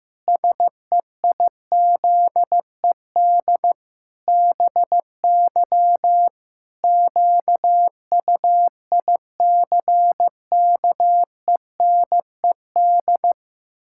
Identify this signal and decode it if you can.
Morse code, heard as a keyed tone, about 14 wpm, 700 Hz. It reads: SEIZED BY QUICKENED
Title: Pangram Challenge v2 - 15wpm